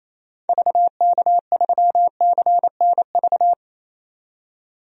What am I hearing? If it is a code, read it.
Morse code, heard as a keyed tone, about 28 wpm, 700 Hz. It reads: VK3CNV
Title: DXCC-100 Call Signs - 6 Characters - 28wpm